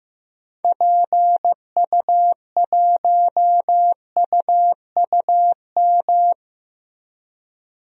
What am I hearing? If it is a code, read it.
Morse code, heard as a keyed tone, about 15 wpm, 700 Hz. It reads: PU1UUM
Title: DXCC-100 Call Signs - 6 Characters - Repeated 2x - 15wpm